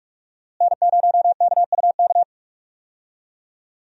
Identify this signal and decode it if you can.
Morse code, heard as a keyed tone, about 47 wpm, 700 Hz. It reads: N0KUK
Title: USA Call Signs - 5 Characters - Repeated 2x - 45wpm